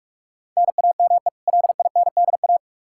Morse code, heard as a keyed tone, about 46 wpm, 700 Hz. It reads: NAME LINDA
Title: Practice QSO - Exchange 1 - Beginning - 45wpm